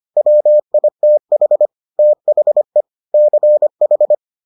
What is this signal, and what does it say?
Morse code, heard as a keyed tone, about 25 wpm, 600 Hz. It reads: WITH THE CH